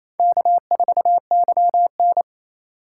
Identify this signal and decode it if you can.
Morse code, heard as a keyed tone, about 36 wpm, 700 Hz. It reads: K4YN